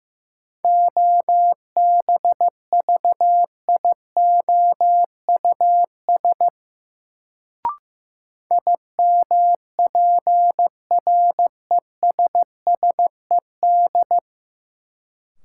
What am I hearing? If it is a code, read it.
Morse code, heard as a keyed tone, about 15 wpm, 700 Hz. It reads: OBVIOUS IMPRESSED
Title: Sentences from Top 1600 Words - New Word Drill Alt - 15wpm